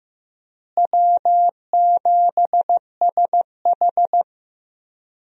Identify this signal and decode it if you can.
Morse code, heard as a keyed tone, about 15 wpm, 700 Hz. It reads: W7SH
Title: USA Call Signs - 4 Characters - Repeated 3x - 15wpm